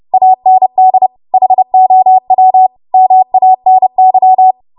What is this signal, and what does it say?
Morse code, heard as a keyed tone, about 30 wpm, 750 Hz. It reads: AND HOW MANY
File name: Morse / 30WPM